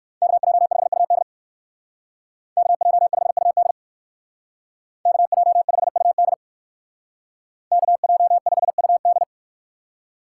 Morse code, heard as a keyed tone, about 58 wpm, 700 Hz. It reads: KJ5UD KJ5UD KJ5UD KJ5UD